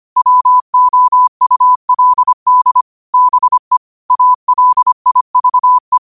Morse code, heard as a keyed tone, about 25 wpm, 1000 Hz. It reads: WOULD BE ALIVE
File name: qotdfiles / 25wpm